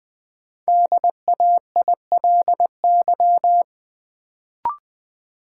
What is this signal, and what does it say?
Morse code, heard as a keyed tone, about 20 wpm, 700 Hz. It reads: DAILY
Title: Sentences from Top 1100 Words - New Word Drill - 20wpm